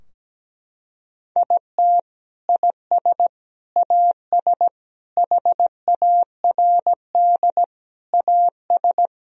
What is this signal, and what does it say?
Morse code, heard as a keyed tone, about 17 wpm, 700 Hz. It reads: IT IS AS HARD AS
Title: Sentences from Top 1000 Words - Review Part 2 - 17wpm